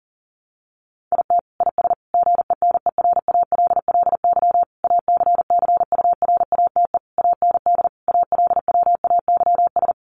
Morse code, heard as a keyed tone, about 37 wpm, 700 Hz. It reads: IT IS GENERALLY ACCURATE AND ALWAYS
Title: Sentences from Top 1200 Words 40wpm